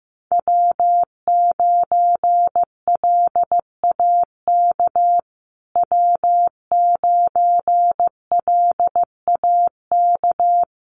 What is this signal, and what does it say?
Morse code, heard as a keyed tone, about 15 wpm, 700 Hz. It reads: W9LAK W9LAK
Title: USA Call Signs Repeated 3x 15wpm